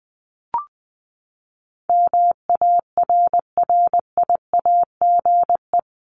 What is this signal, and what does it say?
Morse code, heard as a keyed tone, about 20 wpm, 700 Hz. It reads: MARRIAGE
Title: Top 751-1000 Words 20wpm